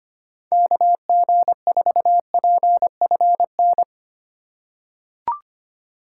Morse code, heard as a keyed tone, about 25 wpm, 700 Hz. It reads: KG4PFN
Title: USA Call Signs - 6 Characters - 4x Character Spacing - 25wpm